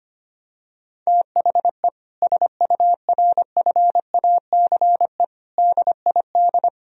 Morse code, heard as a keyed tone, about 25 wpm, 700 Hz. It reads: THE SURFACE DID
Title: Sentences from Top 1000 Words - Review Part 3 - 25wpm